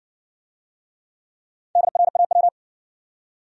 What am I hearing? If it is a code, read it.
Morse code, heard as a keyed tone, about 60 wpm, 700 Hz. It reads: DRAW